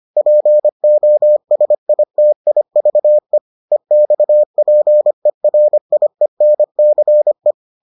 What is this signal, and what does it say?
Morse code, heard as a keyed tone, about 25 wpm, 600 Hz. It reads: POSITIVE EXPERIENCE